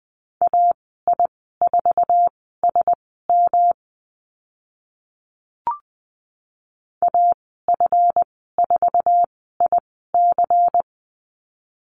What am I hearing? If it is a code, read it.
Morse code, heard as a keyed tone, about 20 wpm, spaced as for 14 wpm, 700 Hz. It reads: AI4SM AF4IC
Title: 161 Lesson 26g - Letter g - Call Signs 20wpm